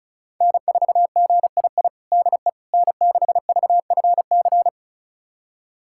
Morse code, heard as a keyed tone, about 36 wpm, 700 Hz. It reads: N4GII DE N6VFC
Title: USA Callsign DE Callsign - 5 Characters - 35wpm